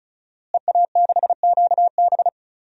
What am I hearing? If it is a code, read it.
Morse code, heard as a keyed tone, about 34 wpm, 700 Hz. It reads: EA6QB
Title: DXCC-100 Call Signs - 5 Characters - Repeated 2x - 35wpm